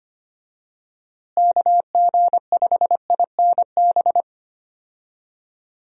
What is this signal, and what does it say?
Morse code, heard as a keyed tone, about 25 wpm, 700 Hz. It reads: KG5INB
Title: USA Call Signs - 6 Characters - Repeated 2x - 25wpm